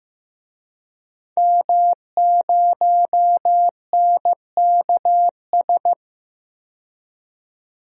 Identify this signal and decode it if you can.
Morse code, heard as a keyed tone, about 15 wpm, 700 Hz. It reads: M0NKS
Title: DXCC-100 Call Signs - 5 Characters - 15wpm